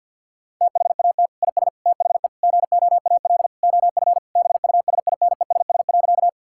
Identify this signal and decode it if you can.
Morse code, heard as a keyed tone, about 52 wpm, 690 Hz. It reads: THAT IS THE GOAL OF BUSINESS.
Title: Sentences from Top 1000 Words - Review Part 5 - 50wpm